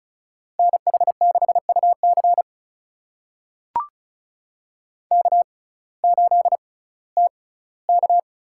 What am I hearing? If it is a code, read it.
Morse code, heard as a keyed tone, about 36 wpm, spaced as for 19 wpm, 700 Hz. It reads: NH6UC K8TK
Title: USA Call Signs - 6x Character Spacing - 35wpm